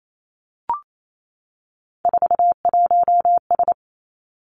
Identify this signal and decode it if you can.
Morse code, heard as a keyed tone, about 28 wpm, 700 Hz. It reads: 41S
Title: Three Challenging Letter-Number Combinations 28wpm